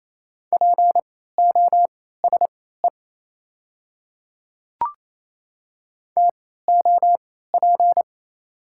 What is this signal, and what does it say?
Morse code, heard as a keyed tone, about 28 wpm, spaced as for 16 wpm, 690 Hz. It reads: POSE TOP
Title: 3x Character Spacing - Top 500 Words - 28wpm